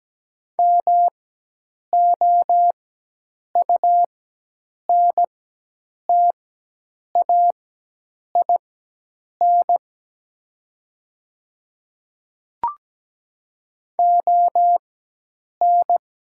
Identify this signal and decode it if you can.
Morse code, heard as a keyed tone, about 17 wpm, spaced as for 8 wpm, 700 Hz. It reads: MOUNTAIN ON
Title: 4x Character Spacing - Top 500 Words - 17wpm